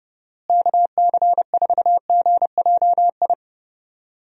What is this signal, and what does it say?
Morse code, heard as a keyed tone, about 31 wpm, 700 Hz. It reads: KC4GJI
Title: USA Call Signs - 6 Characters - Repeated 2x - 30wpm